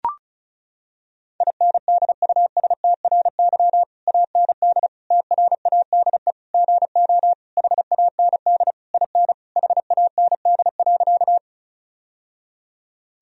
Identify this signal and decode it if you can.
Morse code, heard as a keyed tone, about 36 wpm, 700 Hz. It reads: INDUSTRY AND TRADE GO HAND IN HAND.
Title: Sentences from Top 1000 Words - Review Part 4 - 35wpm